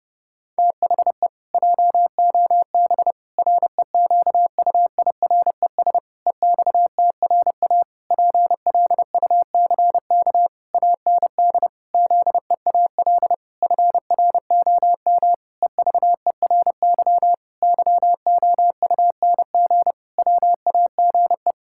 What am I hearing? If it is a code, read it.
Morse code, heard as a keyed tone, about 32 wpm, 700 Hz. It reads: THE JOB REQUIRES EXTRA PLUCK AND ZEAL FROM EVERY YOUNG WAGE
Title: Pangram Challenge v2 - 30wpm